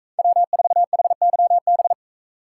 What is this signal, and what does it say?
Morse code, heard as a keyed tone, about 40 wpm, 700 Hz. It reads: W4HYB